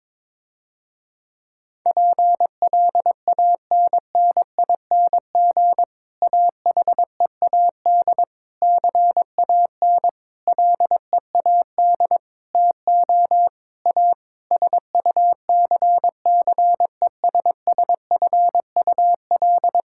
Morse code, heard as a keyed tone, about 21 wpm, 700 Hz. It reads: PLANNING AHEAD CAN LEAD TO A SUCCESSFUL
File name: Difficult Sentences with Hints - 22wpm